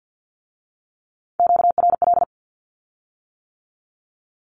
Morse code, heard as a keyed tone, about 50 wpm, 700 Hz. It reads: QRL
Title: Q-Codes with Spoken Meaning - 50wpm